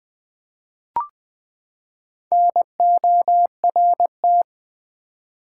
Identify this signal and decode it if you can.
Morse code, heard as a keyed tone, about 20 wpm, 700 Hz. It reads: NORT
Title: Build A Word - 4 Letters At A Time - 8-Letter Words - 20wpm